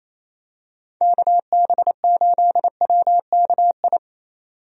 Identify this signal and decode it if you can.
Morse code, heard as a keyed tone, about 27 wpm, 700 Hz. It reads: KB8WKI
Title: USA Call Signs - 6 Characters - 28wpm